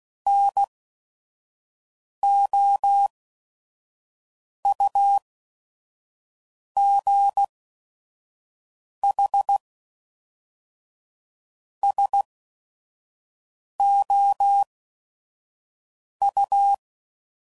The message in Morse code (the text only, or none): N O U G H S O U